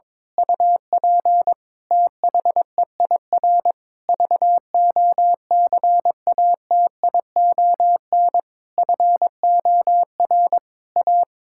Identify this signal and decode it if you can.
Morse code, heard as a keyed tone, about 22 wpm, 700 Hz. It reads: UP THEIR VOCATION FOR A
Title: Pangram Challenge v2 - 22wpm